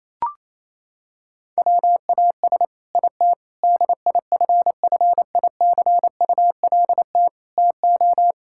Morse code, heard as a keyed tone, about 28 wpm, 700 Hz. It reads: WAS IT DIFFICULT TO
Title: Sentences from Top 1000 Words - Review Part 6 - 28wpm